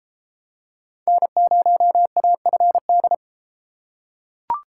Morse code, heard as a keyed tone, about 33 wpm, 700 Hz. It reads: N0AFD